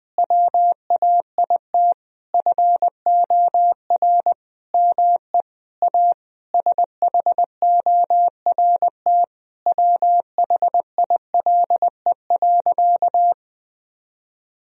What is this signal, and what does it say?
Morse code, heard as a keyed tone, about 20 wpm, 700 Hz. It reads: WAIT FOR ME A SHORT WHILE.
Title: Sentences from Top 1000 Words - Review Part 2 - 20wpm